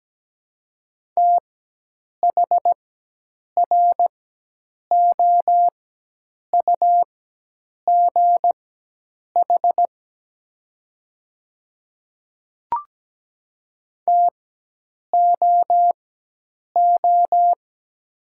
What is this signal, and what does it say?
Morse code, heard as a keyed tone, about 17 wpm, spaced as for 8 wpm, 700 Hz. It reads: THROUGH TOO
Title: 4x Character Spacing - Top 300 Words - 17wpm